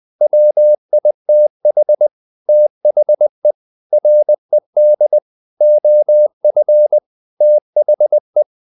Morse code, heard as a keyed tone, about 20 wpm, 600 Hz. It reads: WITH THE RED OF THE
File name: the old man and the sea part 2 20wpm